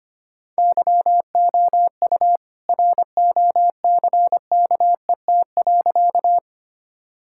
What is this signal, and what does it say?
Morse code, heard as a keyed tone, about 25 wpm, 700 Hz. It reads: YOU ROCKET.
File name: practice / 25wpm / All-Age Appropriate Jokes - 25wpm